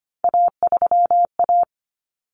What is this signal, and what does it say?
Morse code, heard as a keyed tone, about 25 wpm, 700 Hz. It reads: A3A
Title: Common Cut Numbers - 3-Digit 25wpm